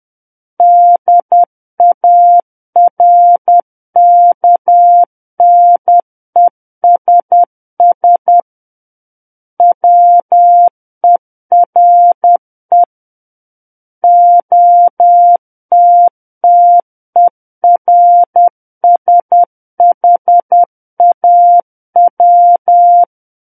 Morse code, heard as a keyed tone, about 10 wpm, 700 Hz. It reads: DARKNESS WERE OTTERSHAW
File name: Book One / 01-Chapter 1 - 10 WPM